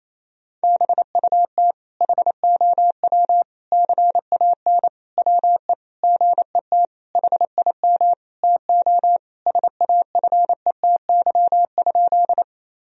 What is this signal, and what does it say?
Morse code, heard as a keyed tone, about 27 wpm, 700 Hz. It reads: BUT HOW CAN WE GET HIM TO SAFETY?
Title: Sentences from Top 1000 Words - Review Part 4 - 28wpm